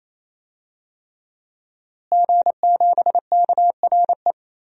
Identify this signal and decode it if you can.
Morse code, heard as a keyed tone, about 28 wpm, 700 Hz. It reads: G7KRE
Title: DXCC-100 Call Signs - 5 Characters - Repeated 3x - 28wpm